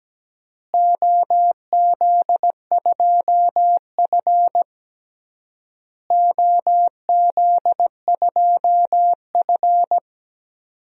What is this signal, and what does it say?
Morse code, heard as a keyed tone, about 17 wpm, 700 Hz. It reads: OZ2F OZ2F
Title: DXCC-100 Call Signs - 4 Characters - Repeated 3x - 17wpm